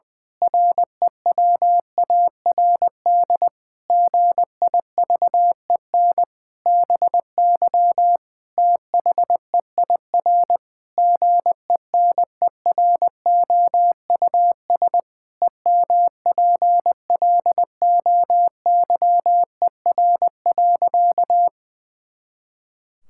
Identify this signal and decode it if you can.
Morse code, heard as a keyed tone, about 20 wpm, 700 Hz. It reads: REWARD GIVEN BY THEIR GENEROUS EMPLOYER.
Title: Pangram Challenge v2 - 20wpm